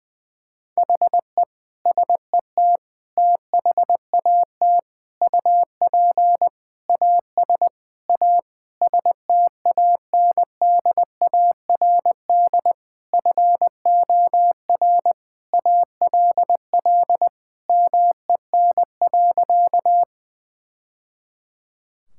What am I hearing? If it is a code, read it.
Morse code, heard as a keyed tone, about 20 wpm, 700 Hz. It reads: HE SET THAT UP AS A STANDARD FOR ALL MEN.
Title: Sentences from Top 1000 Words - Review Part 6 - 20wpm